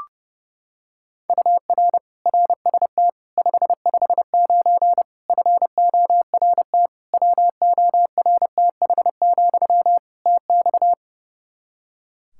Morse code, heard as a keyed tone, about 30 wpm, 700 Hz. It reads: UR RST 559 FORT WORTH, TX
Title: Practice QSO - Exchange 1 - Beginning - 30wpm